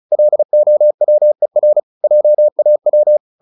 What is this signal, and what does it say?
Morse code, heard as a keyed tone, about 34 wpm, 600 Hz. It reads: LOWER JAW
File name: the old man and the sea 2 35wpm